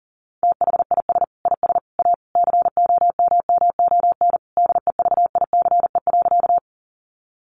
Morse code, heard as a keyed tone, about 40 wpm, 700 Hz. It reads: THIS IS A COMMON DEVICE.
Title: Sentences from Top 1500 Words 40wpm